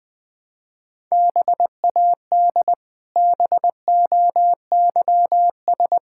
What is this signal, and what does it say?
Morse code, heard as a keyed tone, about 20 wpm, 700 Hz. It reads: BAD BOYS
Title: Pangram Challenge v2 - 20wpm